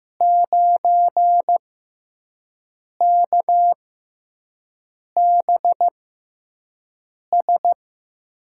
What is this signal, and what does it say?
Morse code, heard as a keyed tone, about 15 wpm, spaced as for 5 wpm, 700 Hz. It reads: 9KBS
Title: USA Call Signs - 5 Characters - 6x Character Spacing - 15wpm